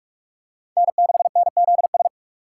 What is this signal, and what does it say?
Morse code, heard as a keyed tone, about 44 wpm, 700 Hz. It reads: N6NZS
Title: USA Call Signs - 5 Characters - 3x Character Spacing - 45wpm